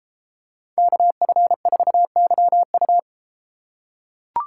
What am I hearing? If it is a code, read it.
Morse code, heard as a keyed tone, about 32 wpm, 700 Hz. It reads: KF4YU